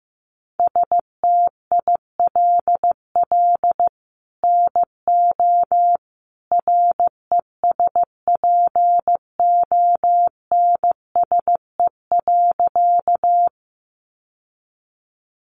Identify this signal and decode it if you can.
Morse code, heard as a keyed tone, about 15 wpm, 700 Hz. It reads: STILL NO RESPONSE.
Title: Sentences from Top 1400 Words 15wpm